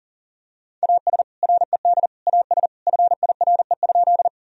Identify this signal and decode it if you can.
Morse code, heard as a keyed tone, about 40 wpm, 700 Hz. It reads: AS RED AS FIRE?
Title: Sentences from Top 1000 Words - Review Part 2 - 40wpm